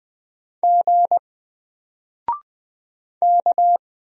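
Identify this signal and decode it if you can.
Morse code, heard as a keyed tone, about 20 wpm, 700 Hz. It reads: G K
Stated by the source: Single letter and number Morse drill at 20 wpm, from N4LCD.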